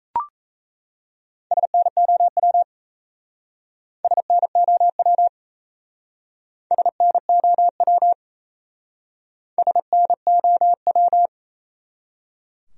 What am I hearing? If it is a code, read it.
Morse code, heard as a keyed tone, about 37 wpm, 700 Hz. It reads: SNOW SNOW SNOW SNOW